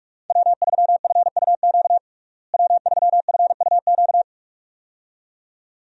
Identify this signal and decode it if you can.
Morse code, heard as a keyed tone, about 43 wpm, 700 Hz. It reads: W3FUQ W3FUQ
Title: USA Call Signs - 5 Characters - Repeated 2x - 45wpm